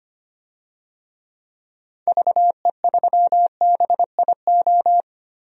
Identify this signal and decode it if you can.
Morse code, heard as a keyed tone, about 25 wpm, 690 Hz. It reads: VE3BIO
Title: DXCC-100 Call Signs - 6 Characters - 25wpm